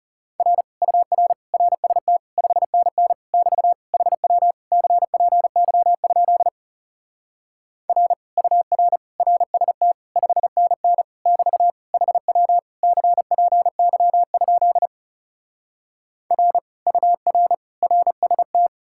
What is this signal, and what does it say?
Morse code, heard as a keyed tone, about 37 wpm, 700 Hz. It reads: R UR RST 5NN <BT> HW CPY? R UR RST 5NN <BT> HW CPY? R UR RST
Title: Speed-Racing Common QSO Phrases - 30wpm